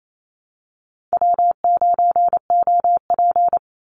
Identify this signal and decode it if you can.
Morse code, heard as a keyed tone, about 28 wpm, 700 Hz. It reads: W9OP
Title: Call Signs - 28wpm